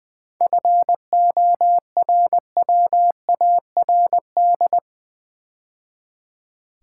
Morse code, heard as a keyed tone, about 20 wpm, 700 Hz. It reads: FORWARD
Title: 7 Letter Words - First 3 Letters Given - 20wpm